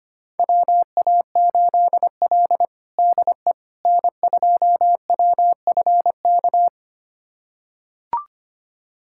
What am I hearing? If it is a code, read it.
Morse code, heard as a keyed tone, about 25 wpm, 700 Hz. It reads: WA8L DE N2WFK
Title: USA Callsign DE Callsign 25wpm